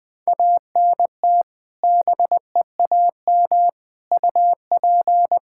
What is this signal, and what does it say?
Morse code, heard as a keyed tone, about 20 wpm, 700 Hz. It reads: ANT BEAM UP
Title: Practice QSO - Exchange 2 - Info - 20wpm